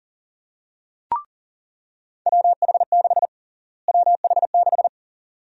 Morse code, heard as a keyed tone, about 40 wpm, 700 Hz. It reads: WH6 WH6
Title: USA Call Sign Prefixes - Repeated 2x - 40wpm